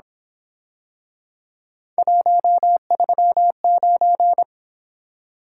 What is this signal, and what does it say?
Morse code, heard as a keyed tone, about 26 wpm, 700 Hz. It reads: 139
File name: Speed-Racing - Three Letter-Number Combinations - 22wpm